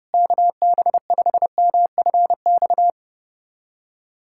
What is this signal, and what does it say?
Morse code, heard as a keyed tone, about 32 wpm, 700 Hz. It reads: KB5MFX
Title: USA Callsign DE Callsign - 6 Characters - 30wpm